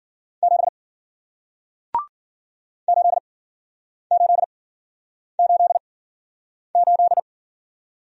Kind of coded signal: Morse code

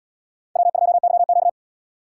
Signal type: Morse code

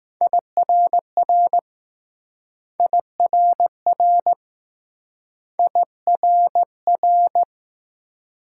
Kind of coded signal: Morse code

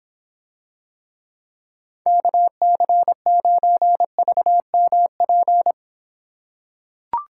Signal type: Morse code

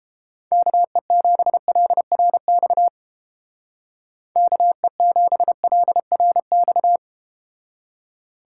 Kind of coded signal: Morse code

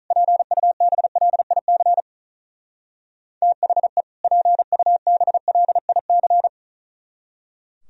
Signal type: Morse code

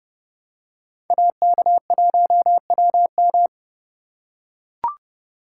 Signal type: Morse code

